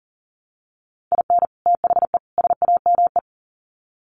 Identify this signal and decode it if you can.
Morse code, heard as a keyed tone, about 40 wpm, 700 Hz. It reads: IN THE SAME
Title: Sets of 3 Words 40wpm